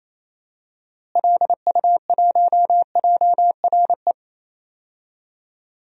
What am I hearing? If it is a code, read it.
Morse code, heard as a keyed tone, about 28 wpm, 700 Hz. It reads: LU1JRE
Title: DXCC-100 Call Signs - 6 Characters - Repeated 3x - 28wpm